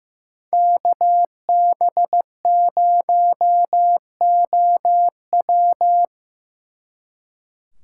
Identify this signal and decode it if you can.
Morse code, heard as a keyed tone, about 17 wpm, 700 Hz. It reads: KB0OW
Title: Speed Racing - USA 2x2 Call Signs 15wpm